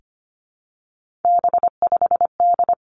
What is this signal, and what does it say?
Morse code, heard as a keyed tone, about 25 wpm, 700 Hz. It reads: B5D
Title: Three Challenging Letter-Number Combinations 25wpm